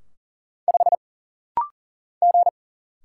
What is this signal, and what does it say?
Morse code, heard as a keyed tone, about 40 wpm, 700 Hz. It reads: 5 G
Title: Single Letter-Number Alt - Rapid-Fire 40wpm